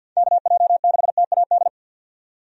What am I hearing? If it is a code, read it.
Morse code, heard as a keyed tone, about 48 wpm, 700 Hz. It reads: KJ6TAD
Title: USA Call Signs - 6 Characters - 50wpm